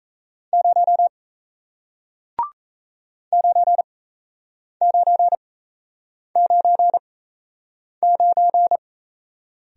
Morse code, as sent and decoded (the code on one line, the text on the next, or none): ----- / ----. ----. ----. ----.
0 9999